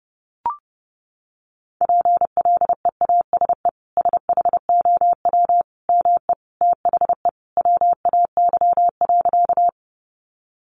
Morse code, heard as a keyed tone, about 30 wpm, 700 Hz. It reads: PLEASE SHOW ME THE WAY.
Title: Sentences from Top 1500 Words 30wpm